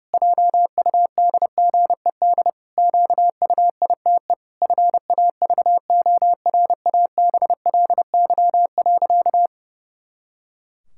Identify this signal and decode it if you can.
Morse code, heard as a keyed tone, about 30 wpm, 700 Hz. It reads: JUDGED QUITE FAVORABLY.
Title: Pangram Challenge v2 - 30wpm